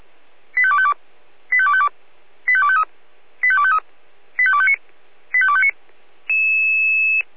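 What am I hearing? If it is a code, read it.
Five-tone selective call (ZVEI), heard as quick runs of five tones, tones of about 70 ms. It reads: 85252 85252 85253 85253 85259 85259
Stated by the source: ZVEI five-tone selective call 85264 received by SDR.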